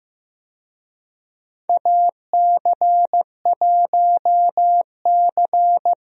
Morse code, heard as a keyed tone, about 15 wpm, 700 Hz. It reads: AC1C
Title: USA Call Signs - 4 Characters - 15wpm